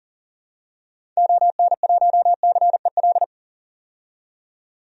Morse code, heard as a keyed tone, about 40 wpm, 700 Hz. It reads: ON1CEL
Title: DXCC-100 Call Signs - 6 Characters - 40wpm